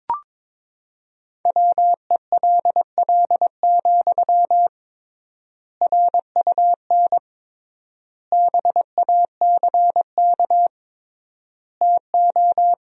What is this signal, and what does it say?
Morse code, heard as a keyed tone, about 22 wpm, 700 Hz. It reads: WELL, RUN BACK TO
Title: 3x Word-Spacing - Sentences from Top 300 Words - 22wpm